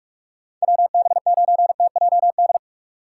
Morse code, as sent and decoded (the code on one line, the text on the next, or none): .-- -... ----. - .--- -..
WB9TJD